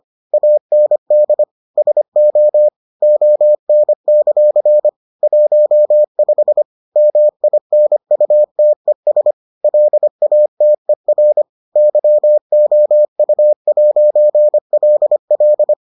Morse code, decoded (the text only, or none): AND SO ON; 15 MINUTES LATER YOU'LL